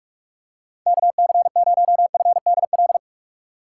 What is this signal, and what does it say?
Morse code, heard as a keyed tone, about 42 wpm, 700 Hz. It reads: KC0FDL
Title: USA Call Signs - 6 Characters - 3x Character Spacing - 45wpm